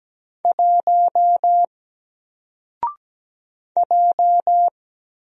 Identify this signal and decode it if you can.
Morse code, heard as a keyed tone, about 17 wpm, 700 Hz. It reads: 1 J
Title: Single Character (Letter:Number:Punctuation:Prosign) 17wpm